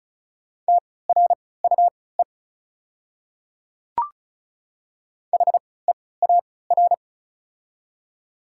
Morse code, heard as a keyed tone, about 36 wpm, spaced as for 20 wpm, 690 Hz. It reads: TRUE HEAR
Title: 3x Character Spacing - Top 500 Words - 35wpm